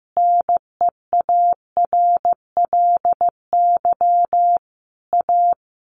Morse code, heard as a keyed tone, about 15 wpm, 700 Hz. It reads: NEARLY A
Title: Sentences from Top 1300 Words 15wpm